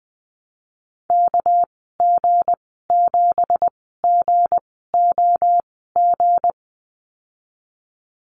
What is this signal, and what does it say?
Morse code, heard as a keyed tone, about 20 wpm, spaced as for 15 wpm, 700 Hz. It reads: KG7GOG
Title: 207 Lesson 33g - Letter j - Call Signs 20wpm